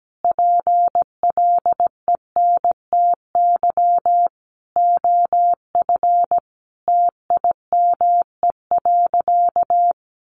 Morse code, heard as a keyed tone, about 17 wpm, 700 Hz. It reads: PLENTY OF TIME.
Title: Sentences from Top 1300 Words 17wpm